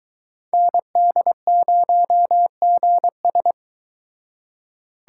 Morse code, heard as a keyed tone, about 22 wpm, 700 Hz. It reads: ND0GS